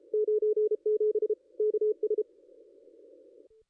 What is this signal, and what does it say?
Morse code, heard as a keyed tone, about 30 wpm, 440 Hz. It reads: 97 KS
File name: contests / ss-cw-k5tr-op-n5rz -0500